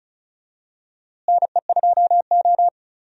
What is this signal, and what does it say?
Morse code, heard as a keyed tone, about 36 wpm, 700 Hz. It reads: NE2O
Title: USA Call Signs - 4 Characters - Repeated 3x - 35wpm